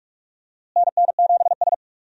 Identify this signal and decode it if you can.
Morse code, heard as a keyed tone, about 46 wpm, 700 Hz. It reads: NN7S
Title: USA Call Signs - 4 Characters - Repeated 2x - 45wpm